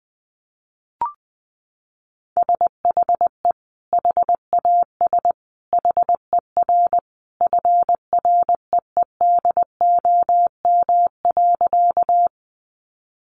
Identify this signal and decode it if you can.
Morse code, heard as a keyed tone, about 20 wpm, 700 Hz. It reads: SHE HAS HER FREEDOM.
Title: Sentences from Top 800 Words 20wpm